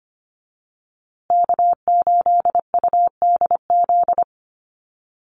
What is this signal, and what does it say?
Morse code, heard as a keyed tone, about 25 wpm, 700 Hz. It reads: K8UDZ
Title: USA Call Signs Repeated 3x 25wpm